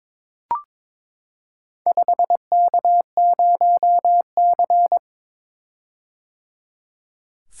Morse code, heard as a keyed tone, about 22 wpm, 700 Hz. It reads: 5K0C
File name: practice / 22wpm / Four Letter-Number Combinations - 22wpm